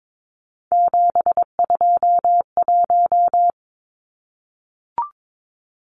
Morse code, heard as a keyed tone, about 22 wpm, 700 Hz. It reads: 721
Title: Numbers - 3-Digit 22wpm